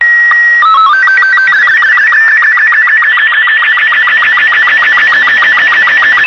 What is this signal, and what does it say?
SSTV (slow-scan television), Robot 36.